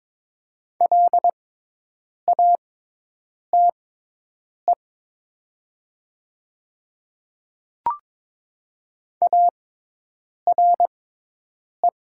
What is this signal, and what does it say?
Morse code, heard as a keyed tone, about 22 wpm, spaced as for 8 wpm, 700 Hz. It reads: LATE ARE
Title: 6x Character Spacing - Top 300 Words - 22wpm